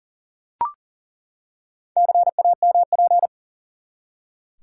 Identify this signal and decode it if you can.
Morse code, heard as a keyed tone, about 40 wpm, 700 Hz. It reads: CAMP
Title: Four Letter Words - 40wpm